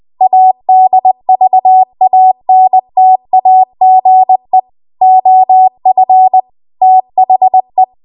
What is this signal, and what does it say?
Morse code, heard as a keyed tone, about 20 wpm, 750 Hz. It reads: ADVANTAGE OF THE
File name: Morse / 20WPM